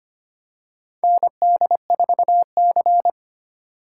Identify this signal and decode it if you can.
Morse code, heard as a keyed tone, about 25 wpm, 700 Hz. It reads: ND4C
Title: USA Call Signs - 2x Character Spacing - 25wpm